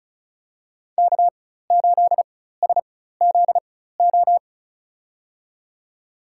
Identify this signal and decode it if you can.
Morse code, heard as a keyed tone, about 36 wpm, spaced as for 16 wpm, 700 Hz. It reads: K8SZO
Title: USA Call Signs - 5 Characters - 4x Character Spacing - 35wpm